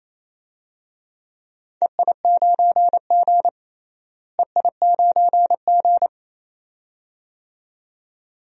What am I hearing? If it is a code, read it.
Morse code, heard as a keyed tone, about 28 wpm, 700 Hz. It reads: EI9G EI9G
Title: DXCC-100 Call Signs - 4 Characters - Repeated 2x - 28wpm